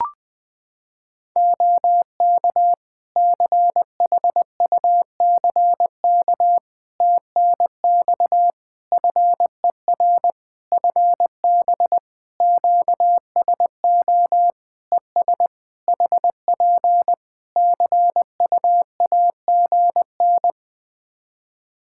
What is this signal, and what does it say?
Morse code, heard as a keyed tone, about 21 wpm, 700 Hz. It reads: OK CHUCK TNX FER FB QSO ES HP CUAGN
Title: Practice QSO - Exchange 4 - Ending - 20wpm